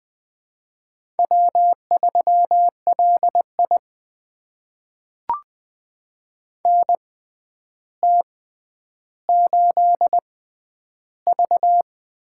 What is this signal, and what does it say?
Morse code, heard as a keyed tone, about 20 wpm, spaced as for 10 wpm, 700 Hz. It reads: W3LI NT8V
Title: USA Call Signs - 4 Characters - 6x Character Spacing - 20wpm